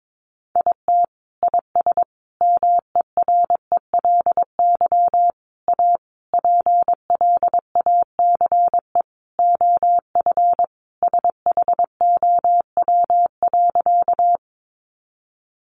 Morse code, heard as a keyed tone, about 22 wpm, 700 Hz. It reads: IT IS MERELY A PLACE OF SHOW.
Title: Sentences from Top 1400 Words 22wpm